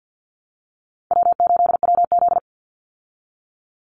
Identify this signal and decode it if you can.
Morse code, heard as a keyed tone, about 50 wpm, 700 Hz. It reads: W8WZ
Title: Easy Call Signs Repeated 3x - 50wpm